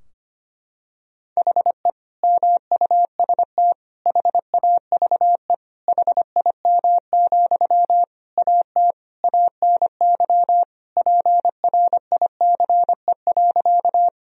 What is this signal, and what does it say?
Morse code, heard as a keyed tone, about 25 wpm, 700 Hz. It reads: HE MUST HAVE HIM, AT ANY PRICE.
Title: Sentences from Top 1000 Words - Review Part 4 - 25wpm